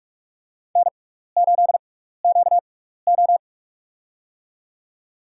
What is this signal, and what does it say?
Morse code, heard as a keyed tone, about 46 wpm, spaced as for 16 wpm, 700 Hz. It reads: N8QO
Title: USA Call Signs - 4 Characters - 6x Character Spacing - 45wpm